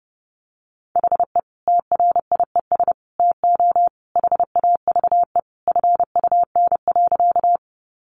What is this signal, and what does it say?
Morse code, heard as a keyed tone, about 30 wpm, 700 Hz. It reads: HE TRIES TO HAVE FUN.
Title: Easy Phrases 30wpm